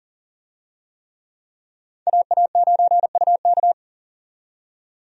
Morse code, heard as a keyed tone, about 40 wpm, 700 Hz. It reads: AA9UK
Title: USA Call Signs - 5 Characters - 40wpm